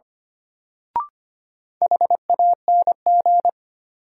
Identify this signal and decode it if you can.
Morse code, heard as a keyed tone, about 25 wpm, 700 Hz. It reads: HANG
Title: Sentences from Top 1600 Words - New Word Drill - 25wpm